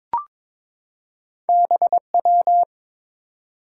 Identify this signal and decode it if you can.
Morse code, heard as a keyed tone, about 22 wpm, 690 Hz. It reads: BW
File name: practice / 22wpm / Two Letter-Number Combinations - 22wpm